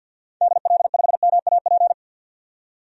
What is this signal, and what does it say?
Morse code, heard as a keyed tone, about 49 wpm, 700 Hz. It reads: DL5MAP
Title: DXCC-100 Call Signs - 6 Characters - 50wpm